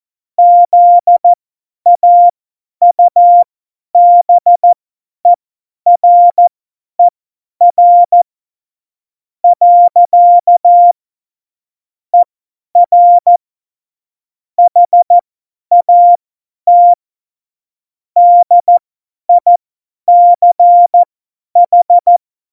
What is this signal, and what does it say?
Morse code, heard as a keyed tone, about 14 wpm, spaced as for 10 wpm, 700 Hz. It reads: ZAUBERER . ER HAT DICH
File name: Morsekurs / Witze / Witz 01. Mai 2018, 50 F70